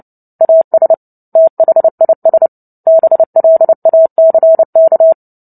Morse code, heard as a keyed tone, about 30 wpm, 650 Hz. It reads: AS THIS BLACK
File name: Book Two / 03-Chapter 3 - 29 WPM